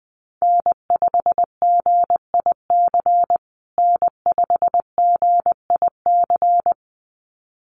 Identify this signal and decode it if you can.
Morse code, heard as a keyed tone, about 20 wpm, 700 Hz. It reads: N5GIC N5GIC
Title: USA Call Signs Repeated 2x 20wpm